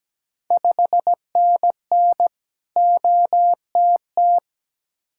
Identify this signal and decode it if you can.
Morse code, heard as a keyed tone, about 17 wpm, 700 Hz. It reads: 5NN OTT
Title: Ontario QSO Party Exchange 17wpm